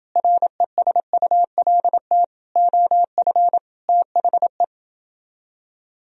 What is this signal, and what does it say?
Morse code, heard as a keyed tone, about 30 wpm, 700 Hz. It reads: RESULT OF THE